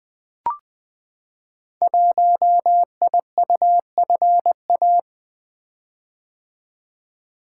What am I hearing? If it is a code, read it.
Morse code, heard as a keyed tone, about 20 wpm, 700 Hz. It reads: 1IUFA
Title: Code Groups - 5 Letter:Numbers - 20wpm